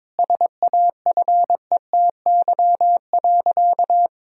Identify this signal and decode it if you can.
Morse code, heard as a keyed tone, about 22 wpm, 700 Hz. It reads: SAFETY.
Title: Sentences from Top 1000 Words - Review Part 4 - 22wpm